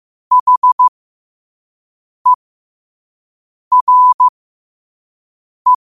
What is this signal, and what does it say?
Morse code, heard as a keyed tone, about 15 wpm, spaced as for 5 wpm, 1000 Hz. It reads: HERE
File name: qotdfiles / 5wpm